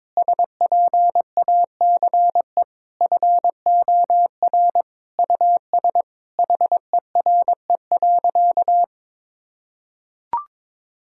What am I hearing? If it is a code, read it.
Morse code, heard as a keyed tone, about 22 wpm, 700 Hz. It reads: SPACE FOR US HERE.
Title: Sentences from Top 1000 Words - Review Part 5 - 22wpm